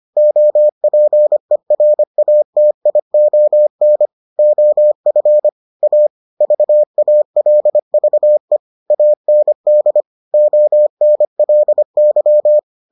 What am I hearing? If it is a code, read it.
Morse code, heard as a keyed tone, about 25 wpm, 600 Hz. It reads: OPERATION OF A VALVE AND ONLY